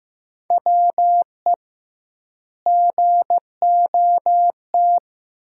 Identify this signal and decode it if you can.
Morse code, heard as a keyed tone, about 15 wpm, 700 Hz. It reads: WE GOT
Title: 2x Word-Spacing - Sentences from Top 300 Words - 15wpm